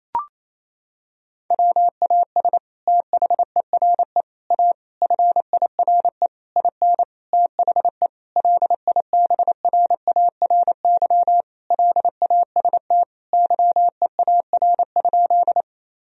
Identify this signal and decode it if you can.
Morse code, heard as a keyed tone, about 28 wpm, 700 Hz. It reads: WAS THERE A FIRE IN THE LIBRARY LAST YEAR?
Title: Sentences from Top 1000 Words - Review Part 4 - 28wpm